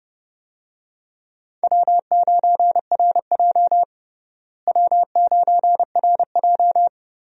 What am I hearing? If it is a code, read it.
Morse code, heard as a keyed tone, about 31 wpm, 700 Hz. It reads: W9RJ W9RJ
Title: USA Call Signs - 4 Characters - Repeated 3x - 30wpm